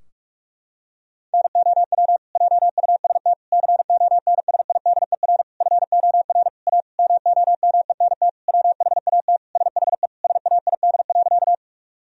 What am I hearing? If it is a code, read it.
Morse code, heard as a keyed tone, about 41 wpm, 700 Hz. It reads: NOW JUST CONSIDER FOR A MOMENT WHAT SHE SAID.
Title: Sentences from Top 1000 Words - Review Part 4 - 45wpm